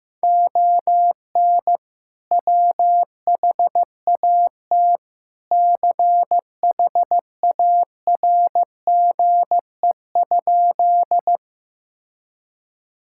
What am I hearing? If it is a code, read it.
Morse code, heard as a keyed tone, about 15 wpm, 700 Hz. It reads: ON WHAT CHARGE?
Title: Sentences from Top 1000 Words - Review Part 6 - 15wpm